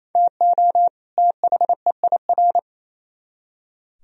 Morse code, heard as a keyed tone, about 36 wpm, 700 Hz. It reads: TO THEIR